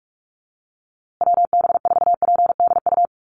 Morse code, heard as a keyed tone, about 44 wpm, 700 Hz. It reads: WB4PDU
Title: USA Call Signs Repeated 2x 45wpm